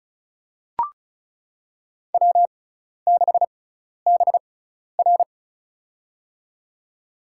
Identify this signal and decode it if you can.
Morse code, heard as a keyed tone, about 36 wpm, spaced as for 12 wpm, 700 Hz. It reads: W6BR
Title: USA Call Signs - 6x Character Spacing - 35wpm